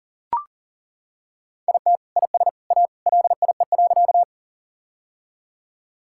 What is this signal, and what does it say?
Morse code, heard as a keyed tone, about 40 wpm, 700 Hz. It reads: IT IS A LIE.
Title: Sentences from Top 1600 Words 40wpm